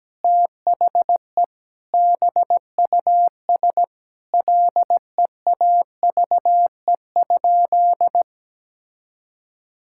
Morse code, heard as a keyed tone, about 17 wpm, 700 Hz. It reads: THE BUS LEAVE?
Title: Sentences from Top 1000 Words - Review Part 6 - 17wpm